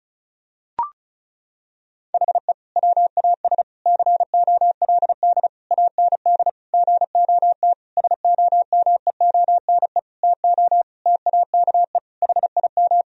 Morse code, heard as a keyed tone, about 36 wpm, 700 Hz. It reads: HE WAS COLD AND GOT SOMEONE TO TAKE HIM
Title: Sentences from Top 1000 Words - Review Part 4 - 35wpm